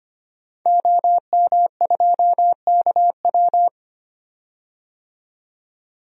Morse code, heard as a keyed tone, about 25 wpm, 700 Hz. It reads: OM2KW
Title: DXCC-100 Call Signs - 5 Characters - 25wpm